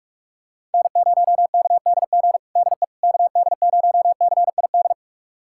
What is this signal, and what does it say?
Morse code, heard as a keyed tone, about 41 wpm, 700 Hz. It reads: N0KDG DE KD0CID
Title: USA Callsign DE Callsign 45wpm